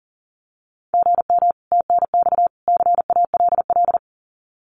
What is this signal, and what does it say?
Morse code, heard as a keyed tone, about 40 wpm, 700 Hz. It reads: GM TNX CALL
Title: Common QSO Phrases 40wpm